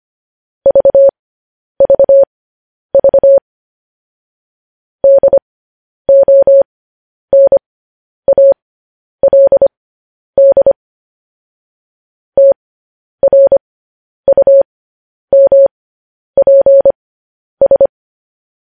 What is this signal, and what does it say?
Morse code, heard as a keyed tone, about 25 wpm, spaced as for 10 wpm, 550 Hz. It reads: VVV DONALD TRUMPS
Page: news headlines in morse code at 10 wpm